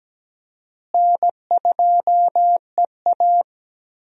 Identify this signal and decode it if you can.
Morse code, heard as a keyed tone, about 17 wpm, 700 Hz. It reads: N2EA
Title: USA Call Signs - 4 Characters - Repeated 3x - 17wpm